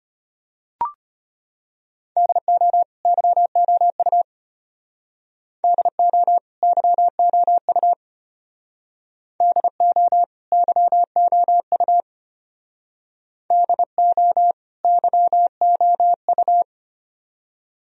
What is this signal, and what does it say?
Morse code, heard as a keyed tone, about 30 wpm, 700 Hz. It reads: DO YOU DO YOU DO YOU DO YOU
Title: Speed-Racing - Sets of 2 Words 25wpm